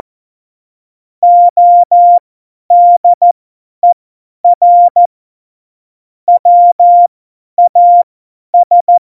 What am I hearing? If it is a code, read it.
Morse code, heard as a keyed tone, about 14 wpm, spaced as for 10 wpm, 700 Hz. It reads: ODER WAS
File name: Morsekurs / Witze / Witz 22. September 2020, 50 F70